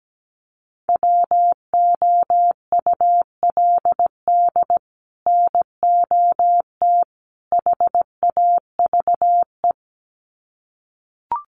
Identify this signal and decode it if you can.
Morse code, heard as a keyed tone, about 17 wpm, 700 Hz. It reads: WOULD NOT HAVE
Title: Sets of 3 Words 17wpm